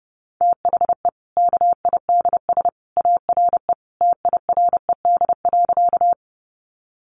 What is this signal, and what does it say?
Morse code, heard as a keyed tone, about 30 wpm, 700 Hz. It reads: THE KIDS ARE TIRED.
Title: Easy Phrases 30wpm